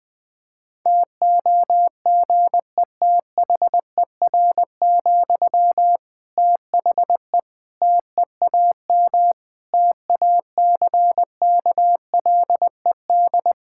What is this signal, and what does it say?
Morse code, heard as a keyed tone, about 21 wpm, 700 Hz. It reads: TOGETHER, THE TEAM TACKLED
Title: Difficult Sentences with Hints - 20wpm